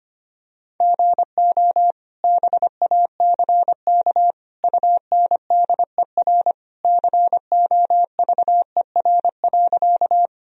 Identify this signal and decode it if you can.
Morse code, heard as a keyed tone, about 25 wpm, 700 Hz. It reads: GO BACK UNDER COVER.
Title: Sentences from Top 1000 Words - Review Part 1 - 25wpm